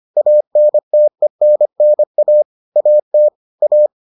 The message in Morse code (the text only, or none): ANTENNA AT A